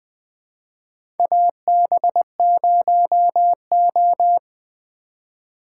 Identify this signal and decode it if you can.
Morse code, heard as a keyed tone, about 20 wpm, 700 Hz. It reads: AB0O
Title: 2023 POTA Activator Call Signs - 20wpm